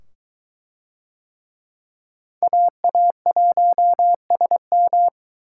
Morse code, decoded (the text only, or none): AA1SM